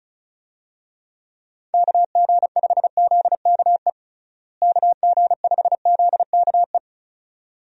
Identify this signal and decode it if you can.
Morse code, heard as a keyed tone, about 36 wpm, 700 Hz. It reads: KG5ZKE KG5ZKE
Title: USA Call Signs - 6 Characters - Repeated 2x - 35wpm